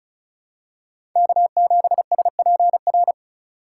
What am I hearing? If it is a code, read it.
Morse code, heard as a keyed tone, about 34 wpm, 700 Hz. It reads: K7SPR
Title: USA Call Signs - 5 Characters - 2x Character Spacing - 35wpm